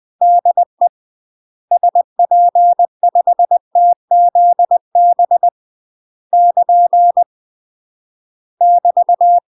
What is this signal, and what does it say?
Morse code, heard as a keyed tone, about 20 wpm, 700 Hz. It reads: DE SP5TZB <KN> <BT>
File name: UK MASTER QSOs TRAINING / 20WPM / UK MASTER QSOs 0063